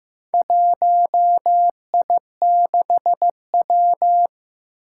Morse code, heard as a keyed tone, about 16 wpm, 700 Hz. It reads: 1I6W
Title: Four Challenging Letter-Number Combinations 15wpm